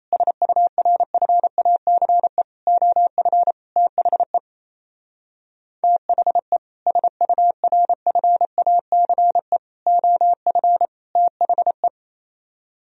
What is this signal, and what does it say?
Morse code, heard as a keyed tone, about 35 wpm, 700 Hz. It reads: SURFACE OF THE THE SURFACE OF THE